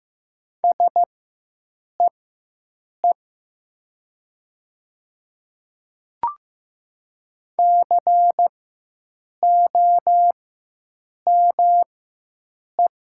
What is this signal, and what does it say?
Morse code, heard as a keyed tone, about 15 wpm, spaced as for 7 wpm, 700 Hz. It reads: SEE COME
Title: 4x Character Spacing - Top 200 Words - 15wpm